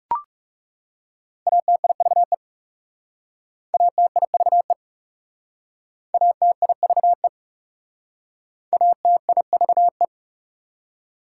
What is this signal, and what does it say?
Morse code, heard as a keyed tone, about 40 wpm, 700 Hz. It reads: ATIVE ATIVE ATIVE ATIVE